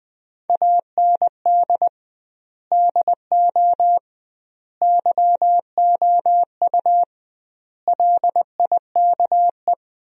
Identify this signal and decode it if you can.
Morse code, heard as a keyed tone, about 20 wpm, 700 Hz. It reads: AND DO YOU LIKE
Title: 2x Word-Spacing - Sentences from Top 100 Words - 20wpm